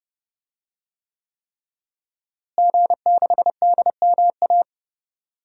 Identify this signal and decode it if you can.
Morse code, heard as a keyed tone, about 32 wpm, 700 Hz. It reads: G6DMA
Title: DXCC-100 Call Signs - 5 Characters - Repeated 3x - 30wpm